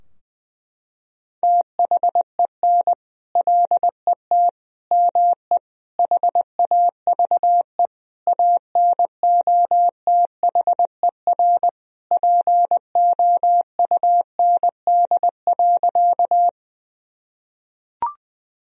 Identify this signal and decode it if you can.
Morse code, heard as a keyed tone, about 20 wpm, 700 Hz. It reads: THEN LET ME HAVE ANOTHER POUND.
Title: Sentences from Top 1000 Words - Review Part 2 - 20wpm